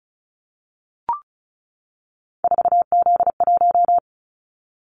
Morse code, heard as a keyed tone, about 34 wpm, 700 Hz. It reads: 4Z1
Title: Callsign Prefixes for Top 50 DXCC - 35wpm